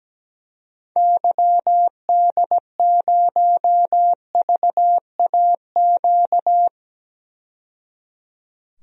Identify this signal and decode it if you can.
Morse code, heard as a keyed tone, about 17 wpm, 700 Hz. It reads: YD0VAQ
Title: DXCC-100 Call Signs - 6 Characters - 17wpm